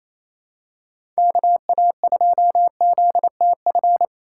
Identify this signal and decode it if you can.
Morse code, heard as a keyed tone, about 28 wpm, 700 Hz. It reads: KA2ZTF
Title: USA Call Signs - 2x Character Spacing - 28wpm